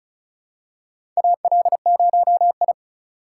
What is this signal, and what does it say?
Morse code, heard as a keyed tone, about 34 wpm, 690 Hz. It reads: AL0I
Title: USA Call Signs - 3x Character Spacing - 35wpm